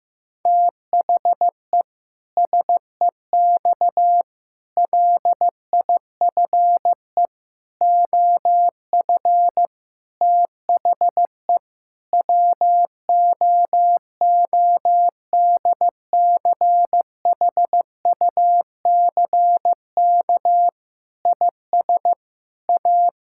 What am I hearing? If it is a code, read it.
Morse code, heard as a keyed tone, about 15 wpm, 700 Hz. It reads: THE SEX LIFE OF THE WOODCHUCK IS A
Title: Pangram Challenge v2 - 15wpm